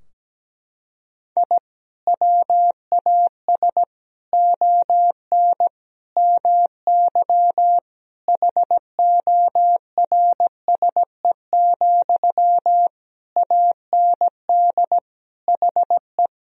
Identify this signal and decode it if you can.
Morse code, heard as a keyed tone, about 17 wpm, 700 Hz. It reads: I WAS ON MY HORSE, AND HE
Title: Sentences from Top 1000 Words - Review Part 3 - 17wpm